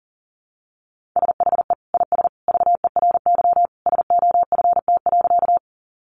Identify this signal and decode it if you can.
Morse code, heard as a keyed tone, about 40 wpm, 700 Hz. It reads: SHE IS VERY SOFT.
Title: Easy Phrases 40wpm